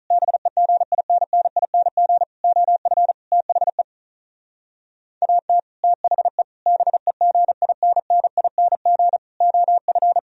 Morse code, heard as a keyed tone, about 47 wpm, 700 Hz. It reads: BEGINNING OF THE AT THE BEGINNING OF